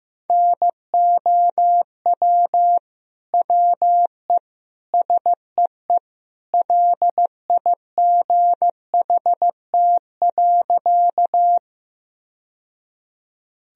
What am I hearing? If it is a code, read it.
Morse code, heard as a keyed tone, about 16 wpm, 700 Hz. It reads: NOW WE SEE LIGHT.
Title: Sentences from Top 1000 Words - Review Part 1 - 15wpm